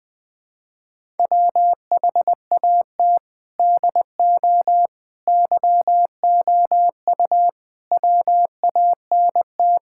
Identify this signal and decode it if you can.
Morse code, heard as a keyed tone, about 21 wpm, 700 Hz. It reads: WHAT DO YOU WANT
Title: Build A Phrase - 4-Word Phrases - 20wpm